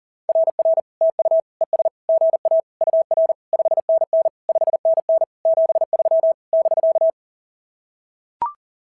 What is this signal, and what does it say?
Morse code, heard as a keyed tone, about 44 wpm, 690 Hz. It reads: RR TU ES GA UR 5NN 5NN 73 <BK>
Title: SOTA QSO Practice 40wpm